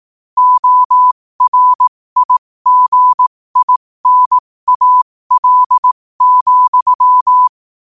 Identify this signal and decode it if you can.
Morse code, heard as a keyed tone, about 18 wpm, spaced as for 15 wpm, 1000 Hz. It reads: ORIGINAL,
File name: qotdfiles / 15wpm